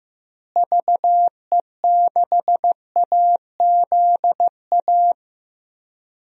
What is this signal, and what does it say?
Morse code, heard as a keyed tone, about 15 wpm, 700 Hz. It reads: VE6AZA
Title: DXCC-100 Call Signs - 6 Characters - Repeated 3x - 15wpm